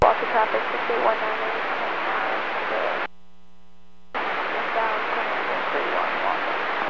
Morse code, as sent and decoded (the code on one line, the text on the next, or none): none